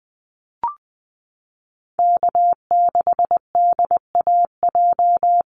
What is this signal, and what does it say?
Morse code, heard as a keyed tone, about 20 wpm, 700 Hz. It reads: K6DAJ
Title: Call Signs Repeated 2x - 20wpm